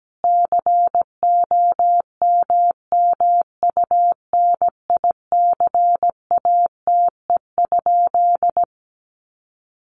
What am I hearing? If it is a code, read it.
Morse code, heard as a keyed tone, about 17 wpm, 700 Hz. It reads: COMMUNICATE?
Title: Sentences from Top 1300 Words 17wpm